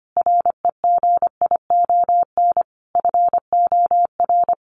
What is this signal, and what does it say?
Morse code, heard as a keyed tone, about 25 wpm, 700 Hz. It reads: REGION FOR
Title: Sentences from Top 1500 Words 25wpm